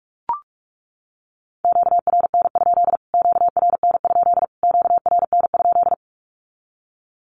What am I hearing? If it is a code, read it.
Morse code, heard as a keyed tone, about 44 wpm, 700 Hz. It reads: QRN? QRN? QRN?
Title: Q-Codes Repeated 3x 45wpm